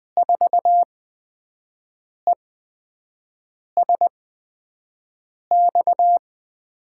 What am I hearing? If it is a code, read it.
Morse code, heard as a keyed tone, about 20 wpm, spaced as for 5 wpm, 700 Hz. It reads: 4ESX
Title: USA Call Signs - 5 Characters - 8x Character Spacing - 20wpm